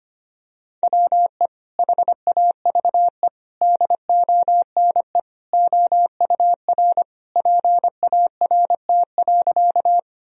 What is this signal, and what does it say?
Morse code, heard as a keyed tone, about 25 wpm, 700 Hz. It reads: WE HAVE DONE OUR PART.
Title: Sentences from Top 1000 Words - Review Part 1 - 25wpm